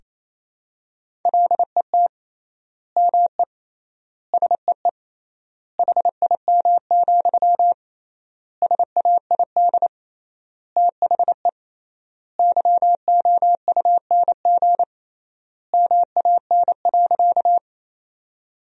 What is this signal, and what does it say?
Morse code, heard as a keyed tone, about 28 wpm, 700 Hz. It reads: LET ME SEE HIM, SAID THE YOUNG MAN.
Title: 3x Word-Spacing - Sentences from Top 400 Words - 28wpm